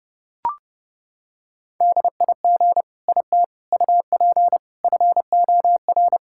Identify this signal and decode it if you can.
Morse code, heard as a keyed tone, about 30 wpm, 700 Hz. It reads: DIG IT UP FOR
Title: Sentences from Top 1600 Words 30wpm